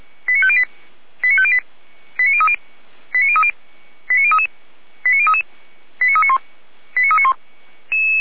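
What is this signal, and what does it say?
Five-tone selective call (ZVEI), heard as quick runs of five tones, tones of about 70 ms. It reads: 89508 89508 89030 89030 89033 89033 89391 89391
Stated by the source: ZVEI five-tone call 89187 received by SDR.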